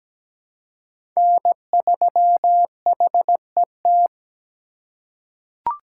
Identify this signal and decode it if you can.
Morse code, heard as a keyed tone, about 17 wpm, 700 Hz. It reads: N3HET
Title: USA Call Signs - 5 Characters - 2x Character Spacing - 17wpm